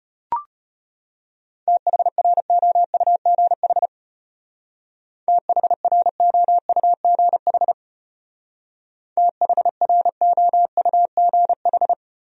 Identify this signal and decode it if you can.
Morse code, heard as a keyed tone, about 33 wpm, 700 Hz. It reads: THROUGH THROUGH THROUGH